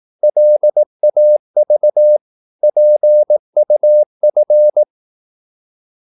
Morse code, heard as a keyed tone, about 18 wpm, 600 Hz. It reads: LAV PUF
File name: Korte ord DK 18wpm